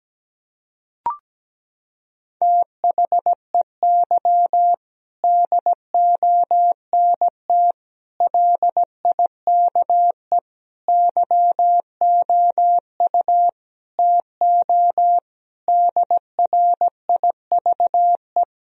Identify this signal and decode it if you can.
Morse code, heard as a keyed tone, about 17 wpm, 700 Hz. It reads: THEY DONT LIKE YOU TO DRIVE
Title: Sentences from Top 1000 Words - Review Part 2 - 17wpm